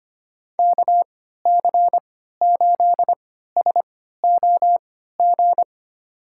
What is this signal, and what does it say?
Morse code, heard as a keyed tone, about 25 wpm, spaced as for 14 wpm, 700 Hz. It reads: KC8SOG
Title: USA Call Signs - 6 Characters - 3x Character Spacing - 25wpm